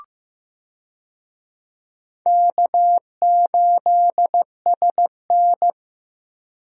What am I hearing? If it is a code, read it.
Morse code, heard as a keyed tone, about 15 wpm, 700 Hz. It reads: K8SN
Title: USA Call Signs - 4 Characters - Repeated 2x - 15wpm